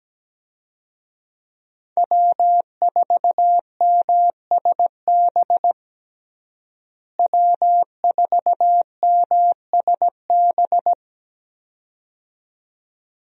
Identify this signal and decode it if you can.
Morse code, heard as a keyed tone, about 17 wpm, 700 Hz. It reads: W4MSB W4MSB
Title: USA Call Signs - 5 Characters - Repeated 2x - 17wpm